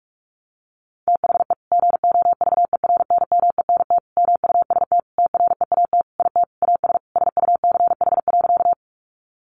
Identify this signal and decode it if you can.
Morse code, heard as a keyed tone, about 44 wpm, 700 Hz. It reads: THE GOVERNMENT MUST TREAT IT AS SUCH.
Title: Sentences from Top 1300 Words 45wpm